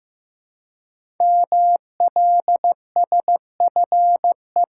Morse code, heard as a keyed tone, about 15 wpm, 700 Hz. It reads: MLSFE
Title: Code Groups - 5 Letters - 15wpm